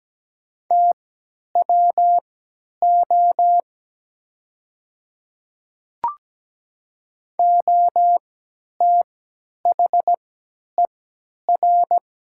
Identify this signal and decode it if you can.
Morse code, heard as a keyed tone, about 17 wpm, spaced as for 10 wpm, 700 Hz. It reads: TWO OTHER
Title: 3x Character Spacing - Top 100 Words - 17wpm